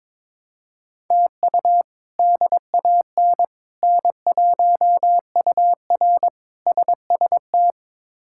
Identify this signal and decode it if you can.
Morse code, heard as a keyed tone, about 22 wpm, 700 Hz. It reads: TU DAN N1UR SST
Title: K1USN SST QSO Practice - 22wpm